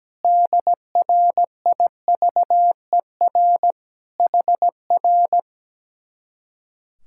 Morse code, heard as a keyed tone, about 17 wpm, 700 Hz. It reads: DRIVER HR
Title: Speed-Racing Common QSO Phrases - 17wpm